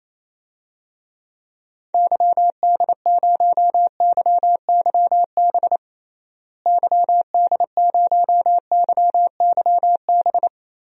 Morse code, heard as a keyed tone, about 28 wpm, 700 Hz. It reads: YD0YYB YD0YYB
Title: DXCC-100 Call Signs - 6 Characters - Repeated 3x - 28wpm